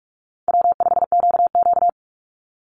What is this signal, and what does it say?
Morse code, heard as a keyed tone, about 44 wpm, 700 Hz. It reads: W5QQ
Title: Call Signs Repeated 3x - 45wpm